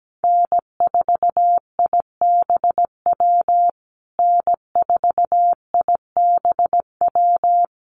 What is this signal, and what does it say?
Morse code, heard as a keyed tone, about 17 wpm, 700 Hz. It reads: N4IBW N4IBW
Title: USA Call Signs Repeated 3x 17wpm